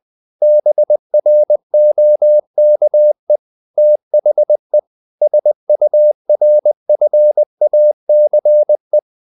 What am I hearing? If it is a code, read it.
Morse code, heard as a keyed tone, about 20 wpm, 600 Hz. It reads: BROKE THE SURFACE
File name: the old man and the sea / the old man and the sea part 1 20wpm